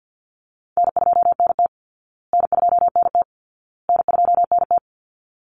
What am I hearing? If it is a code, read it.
Morse code, heard as a keyed tone, about 49 wpm, 700 Hz. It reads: N2NT N2NT N2NT
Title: Easy Call Signs Repeated 3x - 50wpm